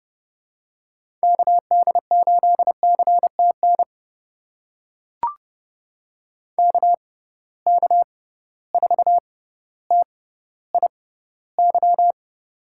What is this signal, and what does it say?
Morse code, heard as a keyed tone, about 32 wpm, spaced as for 16 wpm, 700 Hz. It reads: KD8CTN KK4TIY
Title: USA Call Signs - 6 Characters - 6x Character Spacing - 30wpm